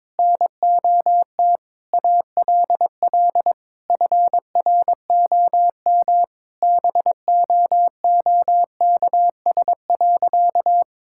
Morse code, heard as a keyed tone, about 22 wpm, 700 Hz. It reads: NOT ALL FROM BOOKS.
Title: Sentences from Top 1000 Words - Review Part 6 - 22wpm